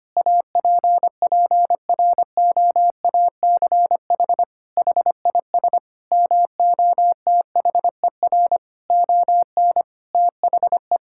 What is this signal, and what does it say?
Morse code, heard as a keyed tone, about 25 wpm, 700 Hz. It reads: APPROACH HIS MOTHER ON THE
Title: Sentences from Top 1000 Words - Review Part 6 - 25wpm